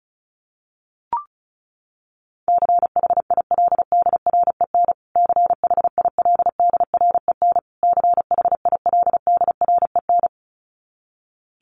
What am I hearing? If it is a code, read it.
Morse code, heard as a keyed tone, about 34 wpm, 700 Hz. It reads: CHILDREN CHILDREN CHILDREN
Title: Top 300 Words Repeated 35wpm